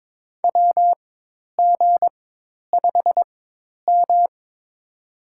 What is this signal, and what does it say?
Morse code, heard as a keyed tone, about 22 wpm, spaced as for 10 wpm, 700 Hz. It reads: WG5M
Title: USA Call Signs - 4x Character Spacing - 22wpm